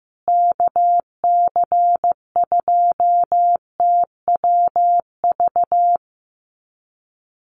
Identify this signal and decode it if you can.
Morse code, heard as a keyed tone, about 14 wpm, 700 Hz. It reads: KC2TWV
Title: USA Call Signs Repeated 3x 15wpm